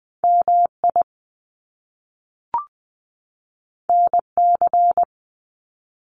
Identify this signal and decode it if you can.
Morse code, heard as a keyed tone, about 20 wpm, 700 Hz. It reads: MI NC
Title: US States 20wpm